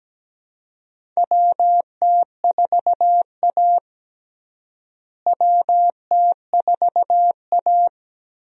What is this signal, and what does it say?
Morse code, heard as a keyed tone, about 17 wpm, 700 Hz. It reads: WT4A WT4A
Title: USA Call Signs - 4 Characters - Repeated 3x - 17wpm